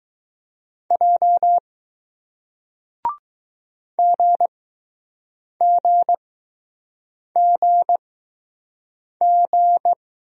Morse code, heard as a keyed tone, about 20 wpm, spaced as for 6 wpm, 700 Hz. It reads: J GGGG